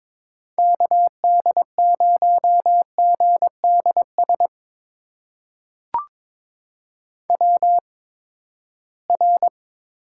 Morse code, heard as a keyed tone, about 22 wpm, spaced as for 15 wpm, 700 Hz. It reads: KD0GDS WR
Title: USA Call Signs - 8x Character Spacing - 22wpm